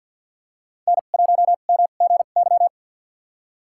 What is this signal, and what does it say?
Morse code, heard as a keyed tone, about 50 wpm, spaced as for 36 wpm, 700 Hz. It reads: N1MGY
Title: USA Call Signs - 5 Characters - 2x Character Spacing - 50wpm